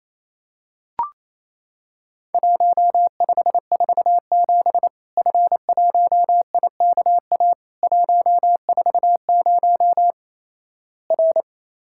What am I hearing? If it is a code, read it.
Morse code, heard as a keyed tone, about 28 wpm, 700 Hz. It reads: 1547 F1IKA 140 R
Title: Worked All Europe DX Contest - QTC Exchange - 28wpm